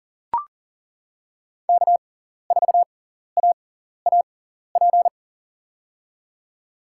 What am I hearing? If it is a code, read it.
Morse code, heard as a keyed tone, about 40 wpm, spaced as for 14 wpm, 700 Hz. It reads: K4AAP
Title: USA Call Signs - 5 Characters - 6x Character Spacing - 40wpm